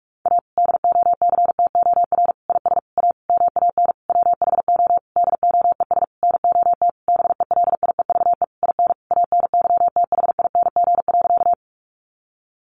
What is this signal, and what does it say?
Morse code, heard as a keyed tone, about 44 wpm, 700 Hz. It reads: A DOCTOR IS A MAN WHO DOES NOT BELIEVE IN ANYTHING.
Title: Sentences from Top 1500 Words 45wpm